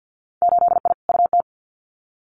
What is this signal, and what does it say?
Morse code, heard as a keyed tone, about 50 wpm, 700 Hz. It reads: 8I UT
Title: Winter Field Day Exchanges Encore 50wpm